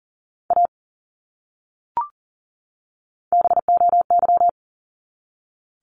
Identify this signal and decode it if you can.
Morse code, heard as a keyed tone, about 40 wpm, 700 Hz. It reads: A BOY
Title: Top 200 Words 40wpm